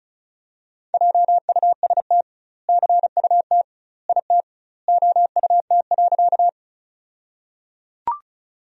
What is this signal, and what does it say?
Morse code, heard as a keyed tone, about 34 wpm, 700 Hz. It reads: JUST CUT IT OUT.
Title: 2x Word-Spacing - Sentences from Top 300 Words - 35wpm